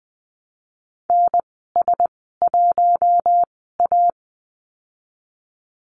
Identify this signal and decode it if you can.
Morse code, heard as a keyed tone, about 20 wpm, spaced as for 14 wpm, 700 Hz. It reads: NS1A
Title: 048 Lesson 9g - Number 4 - Call Signs 20wpm